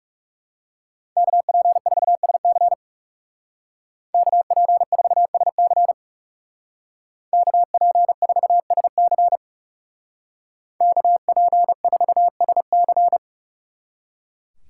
Morse code, heard as a keyed tone, about 36 wpm, 700 Hz. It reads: KP4SC KP4SC KP4SC KP4SC
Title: Speed Racing - USA 2x2 Call Signs 30wpm